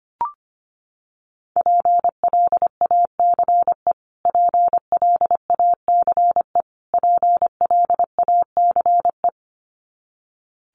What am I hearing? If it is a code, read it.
Morse code, heard as a keyed tone, about 25 wpm, 700 Hz. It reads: PLACE PLACE PLACE
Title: Top 500 Words Repeated 25wpm